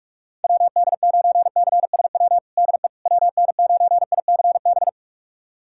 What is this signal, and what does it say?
Morse code, heard as a keyed tone, about 47 wpm, 700 Hz. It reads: WD9CSW DE WN9ICB
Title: USA Callsign DE Callsign 45wpm